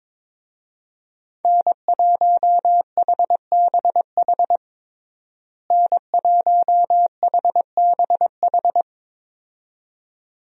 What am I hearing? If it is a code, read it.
Morse code, heard as a keyed tone, about 22 wpm, 700 Hz. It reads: N1HBH N1HBH
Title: USA Call Signs - 5 Characters - Repeated 2x - 22wpm